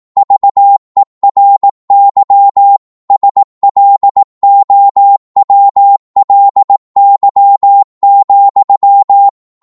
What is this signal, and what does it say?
Morse code, heard as a keyed tone, about 18 wpm, 800 Hz. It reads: VERY SLOWLY,
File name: Morse / psg-18wpm